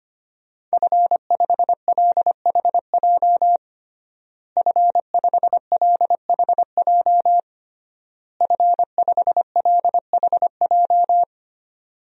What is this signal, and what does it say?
Morse code, heard as a keyed tone, about 25 wpm, 700 Hz. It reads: F5LHJ F5LHJ F5LHJ
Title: DXCC-100 Call Signs - 5 Characters - Repeated 3x - 25wpm